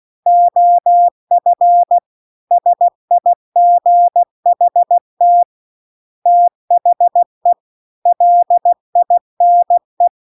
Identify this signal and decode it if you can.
Morse code, heard as a keyed tone, about 16 wpm, 700 Hz. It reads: OF SIGHT THE LINE